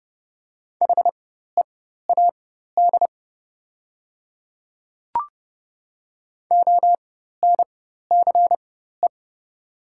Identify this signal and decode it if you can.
Morse code, heard as a keyed tone, about 32 wpm, spaced as for 14 wpm, 700 Hz. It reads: HEAD ONCE
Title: 4x Character Spacing - Top 300 Words - 30wpm